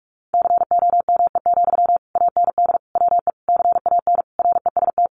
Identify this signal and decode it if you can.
Morse code, heard as a keyed tone, about 44 wpm, 700 Hz. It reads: COME, AND WE CAN REST
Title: Sentences from Top 500 Words 45wpm